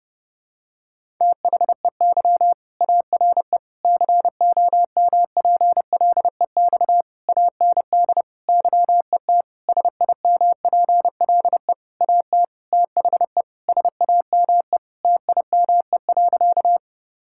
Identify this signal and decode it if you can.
Morse code, heard as a keyed tone, about 30 wpm, 700 Hz. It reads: THEY ARE COMPLEX AND YET SIMPLE AT THE SAME TIME.
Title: Sentences from Top 1000 Words - Review Part 6 - 30wpm